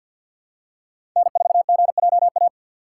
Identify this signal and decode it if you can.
Morse code, heard as a keyed tone, about 53 wpm, 700 Hz. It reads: N4GJA
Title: USA Call Signs - 5 Characters - 2x Character Spacing - 50wpm